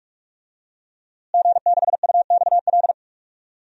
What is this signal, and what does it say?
Morse code, heard as a keyed tone, about 46 wpm, 700 Hz. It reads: G6UXL
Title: DXCC-100 Call Signs - 5 Characters - Repeated 3x - 45wpm